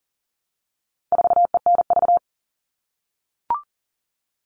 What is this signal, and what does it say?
Morse code, heard as a keyed tone, about 40 wpm, 700 Hz. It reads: 4ENV
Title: All Cut Numbers - 4-Digit 40wpm